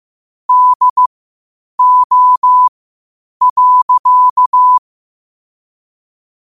Morse code, heard as a keyed tone, about 15 wpm, spaced as for 8 wpm, 1000 Hz. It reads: DO.
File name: qotdfiles / 8wpm